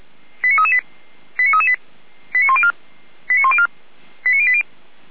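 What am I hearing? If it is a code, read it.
Five-tone selective call (ZVEI), heard as quick runs of five tones, tones of about 70 ms. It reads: 89338 89338 89104 89104 89080
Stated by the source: ZVEI five-tone call received via SDR.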